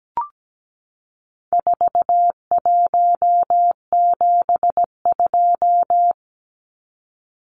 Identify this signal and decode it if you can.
Morse code, heard as a keyed tone, about 17 wpm, 700 Hz. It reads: 4172
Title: Numbers - 4-Digit 17wpm